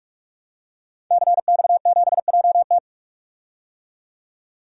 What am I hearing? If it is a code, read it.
Morse code, heard as a keyed tone, about 46 wpm, 700 Hz. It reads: CX7JT
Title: DXCC-100 Call Signs - 5 Characters - 45wpm